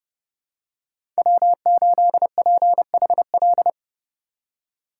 Morse code, heard as a keyed tone, about 29 wpm, 700 Hz. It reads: W8PHL
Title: USA Call Signs - 5 Characters - 2x Character Spacing - 30wpm